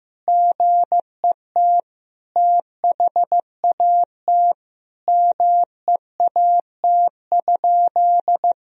Morse code, heard as a keyed tone, about 15 wpm, 700 Hz. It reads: GET THAT MEAT?
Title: Sentences from Top 1000 Words - Review Part 3 - 15wpm